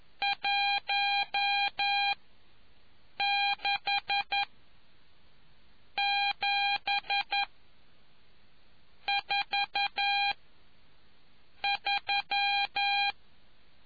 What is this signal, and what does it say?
Morse code, heard as a keyed tone, about 11 wpm, spaced as for 5 wpm, 800 Hz. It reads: 16743